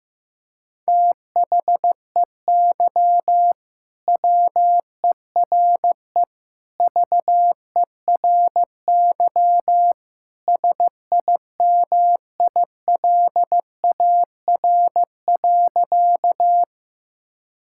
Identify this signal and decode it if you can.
Morse code, heard as a keyed tone, about 16 wpm, 700 Hz. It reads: THEY WERE VERY SIMILAR.
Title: Sentences from Top 1000 Words - Review Part 5 - 15wpm